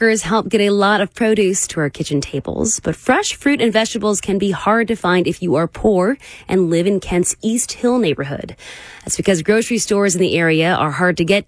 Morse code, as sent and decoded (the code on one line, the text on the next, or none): none